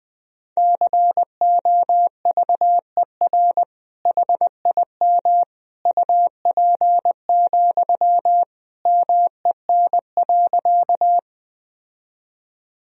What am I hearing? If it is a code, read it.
Morse code, heard as a keyed tone, about 20 wpm, 700 Hz. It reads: COVER HIM UP, MEN.
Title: Sentences from Top 1000 Words - Review Part 1 - 20wpm